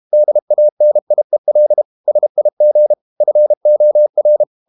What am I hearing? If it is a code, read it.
Morse code, heard as a keyed tone, about 32 wpm, 600 Hz. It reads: DANIEL SIG FOR
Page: Ferie 32 wpm